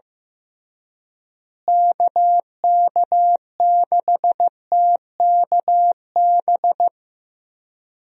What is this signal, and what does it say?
Morse code, heard as a keyed tone, about 15 wpm, 700 Hz. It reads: KK6TKB
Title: USA Call Signs - 6 Characters - Repeated 3x - 15wpm